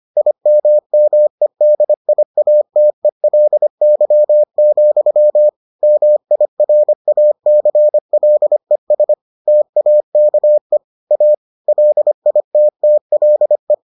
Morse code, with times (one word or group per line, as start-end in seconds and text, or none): 0.2-5.5 IMMEDIATELY,
5.8-9.1 MIRACLES
9.5-10.8 TAKE
11.1-11.3 A
11.7-13.7 LITTLE